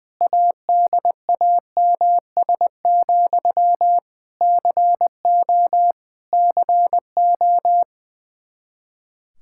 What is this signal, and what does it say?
Morse code, heard as a keyed tone, about 20 wpm, 700 Hz. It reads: ADAMS, CO CO
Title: Practice QSO - Exchange 1 - Beginning - 20wpm